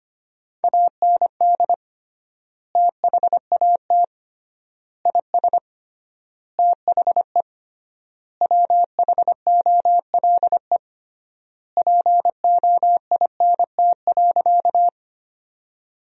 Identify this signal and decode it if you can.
Morse code, heard as a keyed tone, about 25 wpm, 700 Hz. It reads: AND THAT IS THE WHOLE POINT.
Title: 3x Word-Spacing - Sentences from Top 400 Words - 25wpm